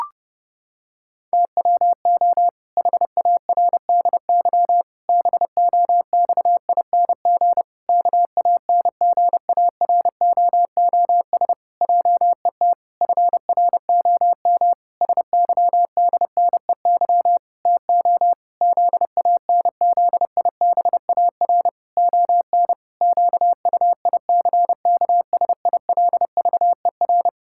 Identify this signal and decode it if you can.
Morse code, heard as a keyed tone, about 30 wpm, 700 Hz. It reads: TWO HARDY BOXING KANGAROOS JET FROM SYDNEY TO ZANZIBAR ON QUICKSILVER
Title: Pangram Challenge v2 - 30wpm